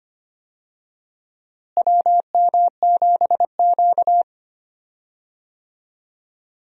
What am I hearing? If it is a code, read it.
Morse code, heard as a keyed tone, about 25 wpm, 700 Hz. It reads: WM7Q
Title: USA Call Signs - 4 Characters - Repeated 3x - 25wpm